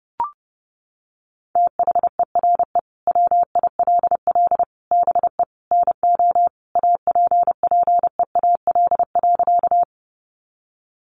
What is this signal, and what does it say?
Morse code, heard as a keyed tone, about 30 wpm, 700 Hz. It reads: THERE WILL BE NO APPEAL.
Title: Sentences from Top 1300 Words 30wpm